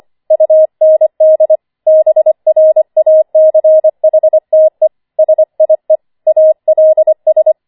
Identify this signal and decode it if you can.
Morse code, heard as a keyed tone, about 25 wpm, 630 Hz. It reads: UND BRACHTE SIE ALS
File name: 25wpm